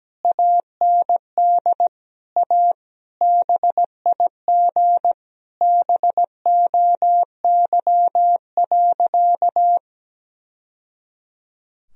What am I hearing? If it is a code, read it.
Morse code, heard as a keyed tone, about 17 wpm, 700 Hz. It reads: AND A BIG BOY.
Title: Sentences from Top 1000 Words - Review Part 1 - 17wpm